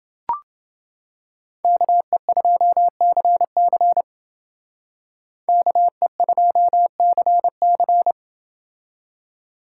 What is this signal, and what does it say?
Morse code, heard as a keyed tone, about 28 wpm, 700 Hz. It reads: KE2CC KE2CC